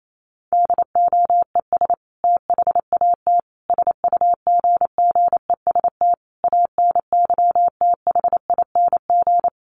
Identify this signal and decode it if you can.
Morse code, heard as a keyed tone, about 28 wpm, 700 Hz. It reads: DOES THAT SUGGEST ANYTHING
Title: Sentences from Top 1300 Words 28wpm